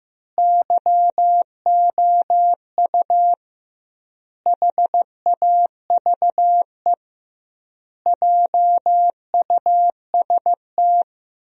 Morse code, heard as a keyed tone, about 15 wpm, 700 Hz. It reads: YOU HAVE JUST
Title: 2x Word-Spacing - Sentences from Top 200 Words - 15wpm